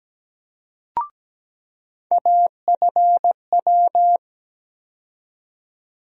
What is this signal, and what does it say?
Morse code, heard as a keyed tone, about 17 wpm, 700 Hz. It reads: AFW
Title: USA Call Sign Postfixes - 3 Letters 17wpm